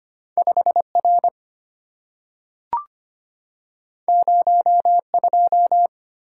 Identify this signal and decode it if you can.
Morse code, heard as a keyed tone, about 25 wpm, 700 Hz. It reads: 5R 02
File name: Two Letter-Number Combinations - 25wpm